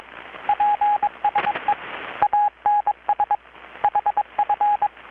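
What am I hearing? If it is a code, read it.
Morse code, heard as a keyed tone, about 22 wpm, 830 Hz. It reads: PSE ANS HF
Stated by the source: Morse from KPH's Night of Nights XX maritime radiotelegraphy event, received in Chiapas, Mexico about 3,800 km away on a Kenwood R-600 with a grounded 10 m long-wire antenna.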